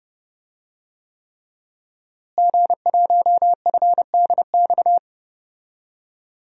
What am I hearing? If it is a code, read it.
Morse code, heard as a keyed tone, about 30 wpm, 700 Hz. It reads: G1FDX
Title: DXCC-100 Call Signs - 5 Characters - Repeated 3x - 30wpm